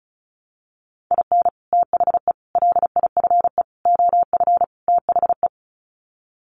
Morse code, heard as a keyed tone, about 34 wpm, 700 Hz. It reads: IN THE LIFE OF THE
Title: Sets of 5 Words Encore2 35wpm